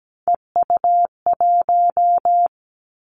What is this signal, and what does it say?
Morse code, heard as a keyed tone, about 17 wpm, 700 Hz. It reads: EU1
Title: Callsign Prefixes for Top 100 DXCC - 17wpm